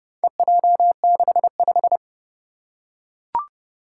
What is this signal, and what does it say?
Morse code, heard as a keyed tone, about 30 wpm, 690 Hz. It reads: EJ65
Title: Four Challenging Letter-Number Combinations 30wpm